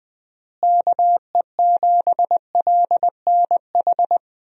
Morse code, heard as a keyed tone, about 20 wpm, 700 Hz. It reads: KE7LNH
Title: USA Call Signs - 6 Characters - Repeated 3x - 20wpm